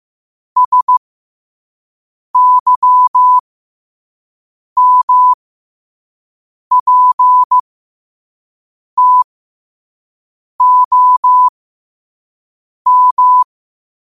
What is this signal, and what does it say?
Morse code, heard as a keyed tone, about 15 wpm, spaced as for 5 wpm, 1000 Hz. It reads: SYMPTOM
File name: qotdfiles / 5wpm